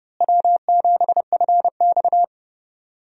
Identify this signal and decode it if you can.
Morse code, heard as a keyed tone, about 30 wpm, 700 Hz. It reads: W7FX
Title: USA Call Signs - 4 Characters - Repeated 3x - 30wpm